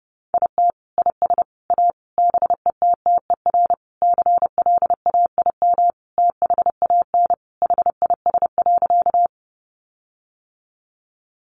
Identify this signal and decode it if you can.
Morse code, heard as a keyed tone, about 30 wpm, 700 Hz. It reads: IT IS A BETTER CLAIM THAN HIS.
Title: Sentences from Top 1500 Words 30wpm